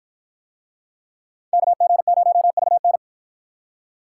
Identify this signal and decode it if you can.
Morse code, heard as a keyed tone, about 53 wpm, 700 Hz. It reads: KG0VN